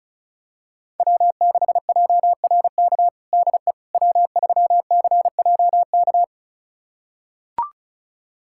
Morse code, heard as a keyed tone, about 34 wpm, 700 Hz. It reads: W6JRK DE W3CJK
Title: USA Callsign DE Callsign - 5 Characters - 35wpm